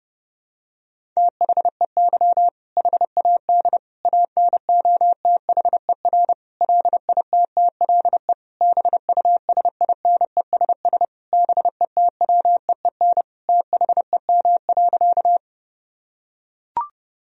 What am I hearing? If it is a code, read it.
Morse code, heard as a keyed tone, about 30 wpm, 700 Hz. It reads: THEY HAD ANOTHER LITTLE BUSINESS BETWEEN THEM.
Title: Sentences from Top 1000 Words - Review Part 1 - 30wpm